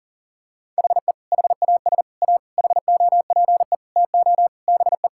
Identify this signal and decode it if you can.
Morse code, heard as a keyed tone, about 40 wpm, 700 Hz. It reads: HE HAS A HOPE TO BE
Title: Sentences from Top 1000 Words - Review Part 6 - 40wpm